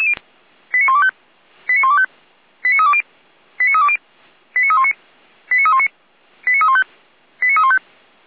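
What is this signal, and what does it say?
Five-tone selective call (ZVEI), heard as quick runs of five tones, tones of about 70 ms. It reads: 89125 89125 89320 89320 89319 89319 89315 89315
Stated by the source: ZVEI five-tone selective call received by SDR.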